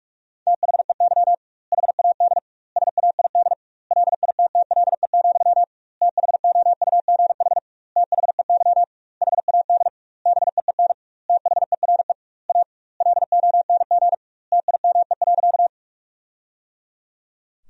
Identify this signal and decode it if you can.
Morse code, heard as a keyed tone, about 46 wpm, 700 Hz. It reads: THEY HAD SAID LITTLE, THOUGH THEY HAD BEEN THERE A LONG TIME.
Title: 2x Word-Spacing - Sentences from Top 400 Words - 45wpm